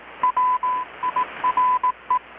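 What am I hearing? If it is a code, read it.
Morse code, heard as a keyed tone, about 18 wpm, 1000 Hz, with radio noise. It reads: WIRE
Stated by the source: Morse from KPH's Night of Nights XX maritime radiotelegraphy event, received in Chiapas, Mexico about 3,800 km away on a Kenwood R-600 with a grounded 10 m long-wire antenna.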